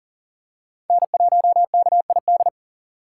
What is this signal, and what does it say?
Morse code, heard as a keyed tone, about 40 wpm, 700 Hz. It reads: N1KID
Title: USA Call Signs - 6x Character Spacing - 40wpm